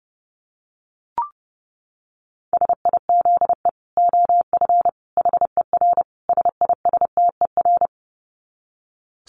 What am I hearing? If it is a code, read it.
Morse code, heard as a keyed tone, about 30 wpm, 700 Hz. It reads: SIZE OF HER SISTER
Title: Dit Challenge 30wpm